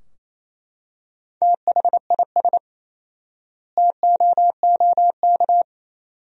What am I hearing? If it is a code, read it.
Morse code, heard as a keyed tone, about 28 wpm, 700 Hz. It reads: THIS TOOK
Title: 4x Word-Spacing - Sentences from Top 300 Words - 28wpm